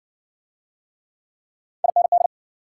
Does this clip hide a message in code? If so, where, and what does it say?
Yes: Morse code from 1.8 to 2.3 s, heard as a keyed tone, about 56 wpm, 700 Hz. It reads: IND